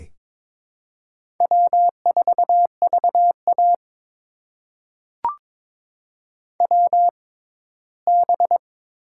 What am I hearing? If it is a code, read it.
Morse code, heard as a keyed tone, about 22 wpm, spaced as for 15 wpm, 700 Hz. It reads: W4VA WB
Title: USA Call Signs - 4 Characters - 6x Character Spacing - 22wpm